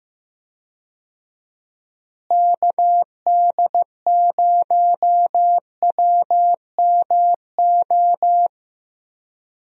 Morse code, heard as a keyed tone, about 16 wpm, 700 Hz. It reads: KD0WMO
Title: USA Call Signs - 6 Characters - Repeated 3x - 15wpm